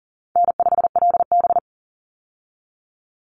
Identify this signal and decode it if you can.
Morse code, heard as a keyed tone, about 40 wpm, 700 Hz. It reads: N5LB
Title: Call Signs - 40wpm